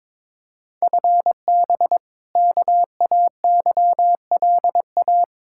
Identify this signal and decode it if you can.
Morse code, heard as a keyed tone, about 22 wpm, 700 Hz. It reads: FB KAYLA
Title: K1USN SST QSO Practice 22wpm